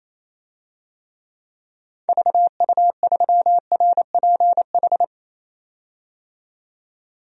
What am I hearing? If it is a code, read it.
Morse code, heard as a keyed tone, about 27 wpm, 700 Hz. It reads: VU3RPH
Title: DXCC-100 Call Signs - 6 Characters - 28wpm